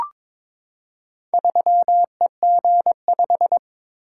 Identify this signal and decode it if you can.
Morse code, heard as a keyed tone, about 22 wpm, 700 Hz. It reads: 3EG5
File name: practice / 22wpm / Four Letter-Number Combinations - 22wpm